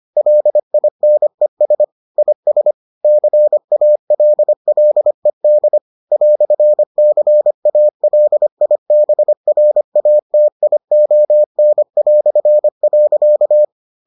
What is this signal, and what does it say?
Morse code, heard as a keyed tone, about 25 wpm, 600 Hz. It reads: LINES IS CALLED "CALIBRATION".